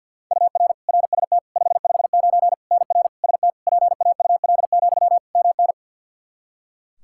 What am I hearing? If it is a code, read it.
Morse code, heard as a keyed tone, about 53 wpm, 700 Hz. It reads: UR RST 559 NR ST PAUL, MN
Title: Practice QSO - Exchange 1 - Beginning - 50wpm